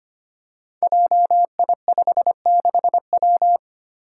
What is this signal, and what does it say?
Morse code, heard as a keyed tone, about 25 wpm, 700 Hz. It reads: JI56W
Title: Five Challenging Letter-Number Combinations 25wpm